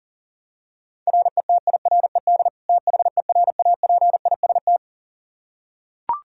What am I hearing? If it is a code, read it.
Morse code, heard as a keyed tone, about 40 wpm, 700 Hz. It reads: RETIRED THERAPIST
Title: Practice QSO - Exchange 3 - More Info - 40wpm